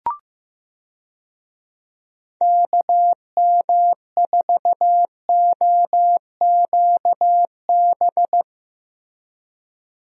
Morse code, heard as a keyed tone, about 15 wpm, 700 Hz. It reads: KM4OQB
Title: USA Call Signs - 6 Characters - Repeated 2x - 15wpm